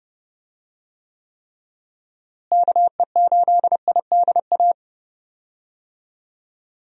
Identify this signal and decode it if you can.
Morse code, heard as a keyed tone, about 30 wpm, 700 Hz. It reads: KE8IDA
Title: USA Call Signs - 6 Characters - Repeated 3x - 30wpm